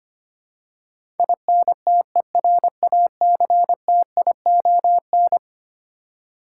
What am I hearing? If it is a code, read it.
Morse code, heard as a keyed tone, about 25 wpm, 700 Hz. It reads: INTERACTION
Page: morse-code-ninja-practice-25wpm